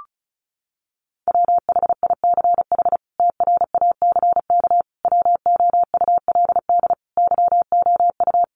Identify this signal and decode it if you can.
Morse code, heard as a keyed tone, about 34 wpm, 700 Hz. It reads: WHICH TRACK WOULD YOU
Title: Sentences from Top 1100 Words 35wpm